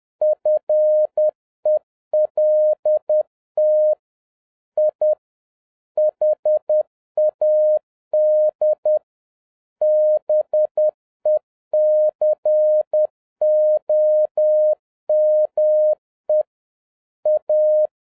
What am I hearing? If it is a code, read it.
Morse code, heard as a keyed tone, about 10 wpm, 610 Hz. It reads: FELT I HAD BECOME A